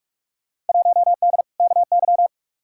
Morse code, heard as a keyed tone, about 43 wpm, 700 Hz. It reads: 1D KY
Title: Field Day Exchanges - 45wpm